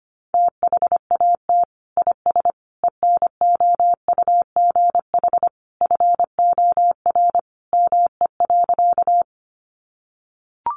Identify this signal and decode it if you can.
Morse code, heard as a keyed tone, about 25 wpm, 700 Hz. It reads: THAT IS ENOUGH FOR ME.
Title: Sentences from Top 400 Words 25wpm